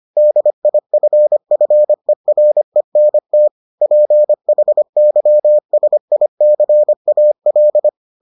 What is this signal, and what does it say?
Morse code, heard as a keyed tone, about 25 wpm, 600 Hz. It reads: DIFFERENT PHYSICAL